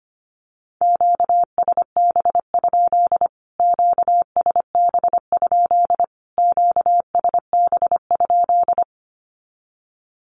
Morse code, heard as a keyed tone, about 25 wpm, 700 Hz. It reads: QSB? QSB? QSB?
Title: Q-Codes Repeated 3x 25wpm